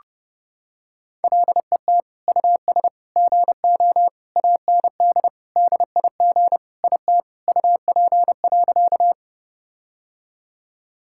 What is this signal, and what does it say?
Morse code, heard as a keyed tone, about 32 wpm, 700 Hz. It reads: LET US GO AND DIG IT UP.
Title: Sentences from Top 1600 Words 30wpm